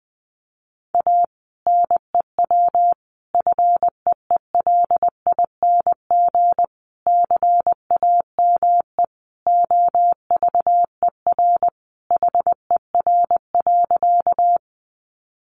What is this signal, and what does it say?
Morse code, heard as a keyed tone, about 20 wpm, 700 Hz. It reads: A NEW FEELING CAME OVER HER.
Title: Sentences from Top 1200 Words 20wpm